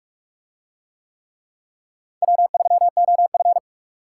Morse code, heard as a keyed tone, about 41 wpm, 700 Hz. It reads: W3OF